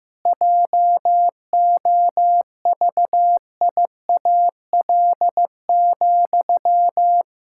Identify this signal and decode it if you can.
Morse code, heard as a keyed tone, about 15 wpm, 700 Hz. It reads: JOVIAL,
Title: Pangram Challenge v2 - 15wpm